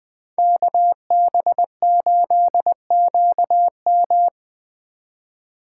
Morse code, heard as a keyed tone, about 20 wpm, 700 Hz. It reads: KB8QM
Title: Speed Racing - USA 2x2 Call Signs 17wpm